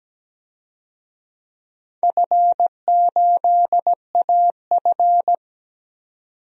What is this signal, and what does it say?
Morse code, heard as a keyed tone, about 17 wpm, 700 Hz. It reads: F8AF
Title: DXCC-100 Call Signs - 4 Characters - 17wpm